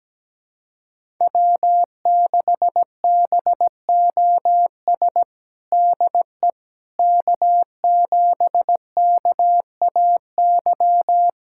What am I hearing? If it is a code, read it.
Morse code, heard as a keyed tone, about 17 wpm, 700 Hz. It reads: W6BOS DE K7KAY
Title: USA Callsign DE Callsign - 5 Characters - 17wpm